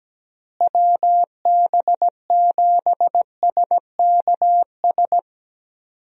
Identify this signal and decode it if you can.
Morse code, heard as a keyed tone, about 17 wpm, 700 Hz. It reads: WB7SKS
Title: USA Call Signs - 6 Characters - Repeated 3x - 17wpm